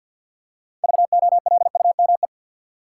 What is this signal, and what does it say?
Morse code, heard as a keyed tone, about 50 wpm, 700 Hz. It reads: VOLUME
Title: Six Letter Words - 50wpm